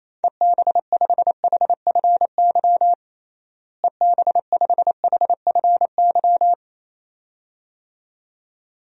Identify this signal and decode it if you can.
Morse code, heard as a keyed tone, about 28 wpm, 700 Hz. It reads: EB5HFY EB5HFY
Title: DXCC-100 Call Signs - 6 Characters - Repeated 2x - 28wpm